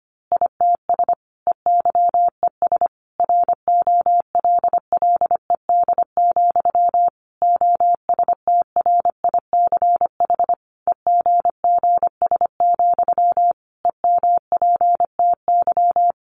Morse code, heard as a keyed tone, about 25 wpm, 700 Hz. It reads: ITS EYES ROLLED, OSTRICH EGGS, EMPTY
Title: A Sound Of Thunder 25wpm